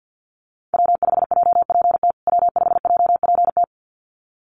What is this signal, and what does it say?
Morse code, heard as a keyed tone, about 49 wpm, 700 Hz. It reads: W5JPT W5JPT
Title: USA Call Signs Repeated 2x 50wpm